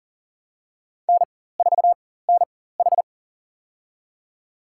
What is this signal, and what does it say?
Morse code, heard as a keyed tone, about 40 wpm, spaced as for 19 wpm, 700 Hz. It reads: N4NH
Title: USA Call Signs - 4 Characters - 4x Character Spacing - 40wpm